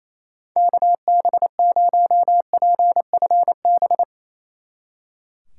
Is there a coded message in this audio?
Yes: Morse code from 0.6 to 4.0 s, heard as a keyed tone, about 37 wpm, 700 Hz. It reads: KB0PFB